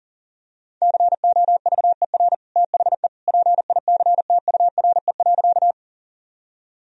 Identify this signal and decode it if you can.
Morse code, heard as a keyed tone, about 40 wpm, 700 Hz. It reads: COVER THE PICTURE.
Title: Sentences from Top 1000 Words - Review Part 1 - 40wpm